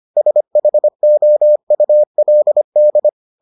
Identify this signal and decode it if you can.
Morse code, heard as a keyed tone, about 25 wpm, 600 Hz. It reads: SHOULD